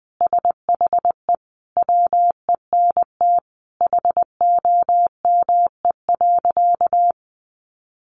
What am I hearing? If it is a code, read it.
Morse code, heard as a keyed tone, about 20 wpm, 700 Hz. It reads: SHE WENT HOME.
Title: Sentences from Top 800 Words 20wpm